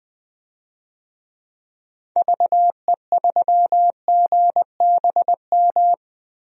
Morse code, heard as a keyed tone, about 20 wpm, 700 Hz. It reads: VE3GBM
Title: DXCC-100 Call Signs - 6 Characters - 20wpm